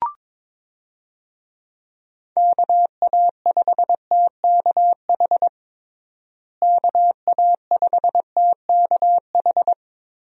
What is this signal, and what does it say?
Morse code, heard as a keyed tone, about 22 wpm, 700 Hz. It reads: KA5TKH KA5TKH
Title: USA Call Signs - 6 Characters - Repeated 2x - 22wpm